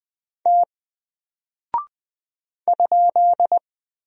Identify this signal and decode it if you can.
Morse code, heard as a keyed tone, about 20 wpm, 700 Hz. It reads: T ?
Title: Single Character (Letter:Number:Punctuation:Prosign) Alt 20wpm